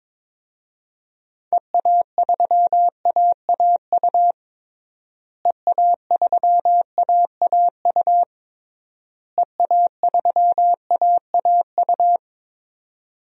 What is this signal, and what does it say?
Morse code, heard as a keyed tone, about 22 wpm, 700 Hz. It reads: EA3AAU EA3AAU EA3AAU
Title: DXCC-100 Call Signs - 6 Characters - Repeated 3x - 22wpm